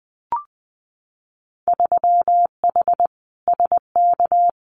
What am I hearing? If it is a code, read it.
Morse code, heard as a keyed tone, about 20 wpm, 700 Hz. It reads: 3H SK
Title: Winter Field Day Exchanges 20wpm